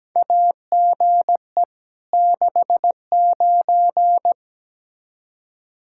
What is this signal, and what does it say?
Morse code, heard as a keyed tone, about 17 wpm, 700 Hz. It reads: AGE 69
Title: Practice QSO - Exchange 3 - More Info - 17wpm